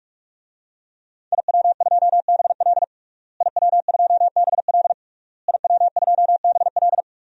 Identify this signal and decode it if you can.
Morse code, heard as a keyed tone, about 46 wpm, 700 Hz. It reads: IW2BL IW2BL IW2BL
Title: DXCC-100 Call Signs - 5 Characters - Repeated 3x - 45wpm